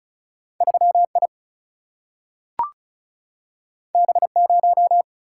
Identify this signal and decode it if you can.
Morse code, heard as a keyed tone, about 36 wpm, 700 Hz. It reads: 3I B0
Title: Single-Letter Single-Number 35wpm